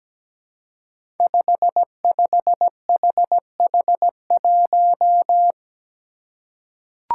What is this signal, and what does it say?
Morse code, heard as a keyed tone, about 17 wpm, 700 Hz. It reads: 55HH1
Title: Five Challenging Letter-Number Combinations 17wpm